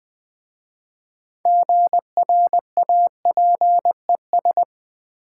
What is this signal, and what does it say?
Morse code, heard as a keyed tone, about 20 wpm, 700 Hz. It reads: GRAPES
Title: Easy Words with Hints - 20wpm